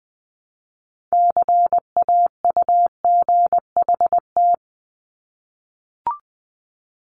Morse code, heard as a keyed tone, about 19 wpm, 700 Hz. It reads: CAUGHT
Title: Multi-letter Phonemes Repeated 3x 20wpm